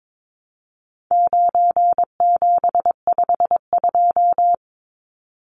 Morse code, heard as a keyed tone, about 22 wpm, 700 Hz. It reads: 9752
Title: Numbers - 4-Digit 22wpm